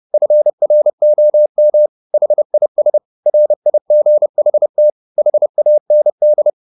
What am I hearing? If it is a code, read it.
Morse code, heard as a keyed tone, about 30 wpm, 600 Hz. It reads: FROM HIS RIGHT HAND